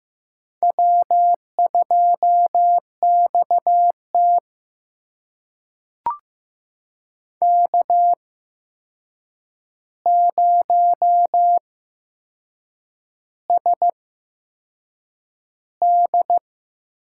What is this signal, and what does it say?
Morse code, heard as a keyed tone, about 15 wpm, spaced as for 6 wpm, 700 Hz. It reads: W2XT K0SD
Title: USA Call Signs - 4 Characters - 8x Character Spacing - 15wpm